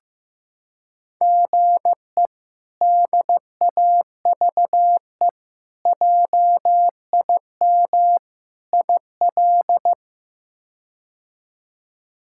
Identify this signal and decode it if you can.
Morse code, heard as a keyed tone, about 15 wpm, 700 Hz. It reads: GE DAVE JIM IL
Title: K1USN SST QSO Practice - 15wpm